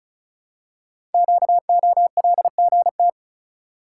Morse code, heard as a keyed tone, about 36 wpm, 690 Hz. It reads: QOLGT
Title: Code Groups - 5 Letters - 35wpm